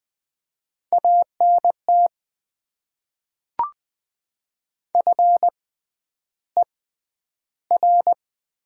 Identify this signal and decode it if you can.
Morse code, heard as a keyed tone, about 20 wpm, spaced as for 10 wpm, 690 Hz. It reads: ANT FER
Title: Top 20 QSO Elements - 6x Character Spacing - 20wpm